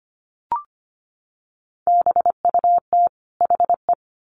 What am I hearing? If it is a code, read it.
Morse code, heard as a keyed tone, about 25 wpm, 700 Hz. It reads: BUT HE
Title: Sets of 2 Words 25wpm